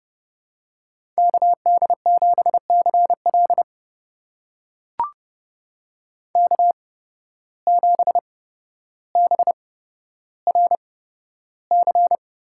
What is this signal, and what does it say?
Morse code, heard as a keyed tone, about 30 wpm, spaced as for 13 wpm, 700 Hz. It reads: KD7CL K7BRC
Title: USA Call Signs - 5 Characters - 8x Character Spacing - 30wpm